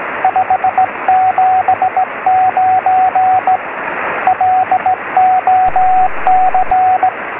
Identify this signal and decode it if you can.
Morse code, heard as a keyed tone, about 17 wpm, 750 Hz, with radio noise. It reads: 579 LOC
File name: sm7lcb / m0bpq-p 23cm